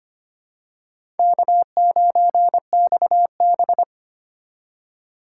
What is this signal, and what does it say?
Morse code, heard as a keyed tone, about 25 wpm, 700 Hz. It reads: K9XB
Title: USA Call Signs - 4 Characters - 3x Character Spacing - 25wpm